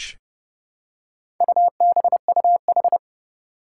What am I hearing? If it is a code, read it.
Morse code, heard as a keyed tone, about 29 wpm, 700 Hz. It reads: UBUH
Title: Four Challenging Letter-Number Combinations 30wpm